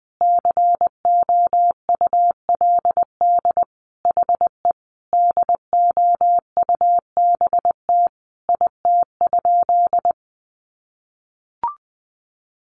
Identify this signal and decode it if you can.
Morse code, heard as a keyed tone, about 20 wpm, 700 Hz. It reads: COULD HE DOUBT IT?
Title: Sentences from Top 1500 Words 20wpm